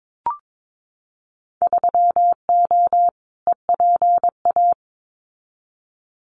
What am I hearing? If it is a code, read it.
Morse code, heard as a keyed tone, about 22 wpm, 700 Hz. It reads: 3O EPA
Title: Winter Field Day Exchanges 22wpm